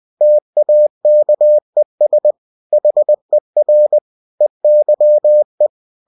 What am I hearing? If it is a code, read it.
Morse code, heard as a keyed tone, about 20 wpm, 600 Hz. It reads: TAKES HER EYE
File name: cw-books / winnie-the-pooh / winnie-20wpm-page-0063